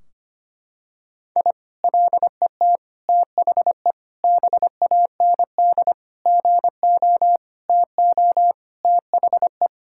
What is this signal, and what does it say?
Morse code, heard as a keyed tone, about 25 wpm, 700 Hz. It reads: I LET THE BAND GO TO THE
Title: Sentences from Top 1600 Words 25wpm